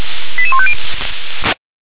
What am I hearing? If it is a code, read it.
Five-tone selective call (ZVEI), heard as quick runs of five tones, tones of about 70 ms. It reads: 88150
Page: /357/SDR/ZVEI/